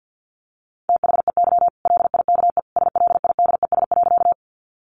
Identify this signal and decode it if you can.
Morse code, heard as a keyed tone, about 50 wpm, 700 Hz. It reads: THEY LIKE SLIDES.
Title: Easy Phrases 50wpm